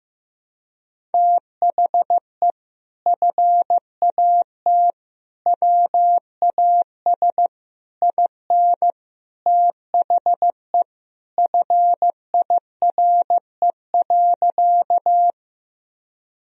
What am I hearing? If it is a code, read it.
Morse code, heard as a keyed tone, about 15 wpm, 700 Hz. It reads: THE FAT WAS IN THE FIRE.
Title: Sentences from Top 1000 Words - Review Part 4 - 15wpm